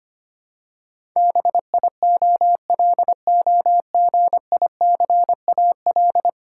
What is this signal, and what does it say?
Morse code, heard as a keyed tone, about 25 wpm, 690 Hz. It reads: BIOLOGICAL
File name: 10 Letter Words - First 3 Letters Given - 25wpm